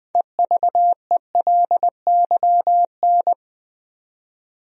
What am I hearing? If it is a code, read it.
Morse code, heard as a keyed tone, about 20 wpm, 700 Hz. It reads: EVELYN
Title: USA Common Names - 20wpm